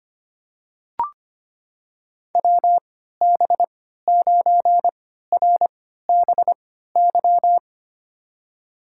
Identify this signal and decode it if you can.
Morse code, heard as a keyed tone, about 25 wpm, spaced as for 14 wpm, 700 Hz. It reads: WB9RBY
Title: USA Call Signs - 6 Characters - 3x Character Spacing - 25wpm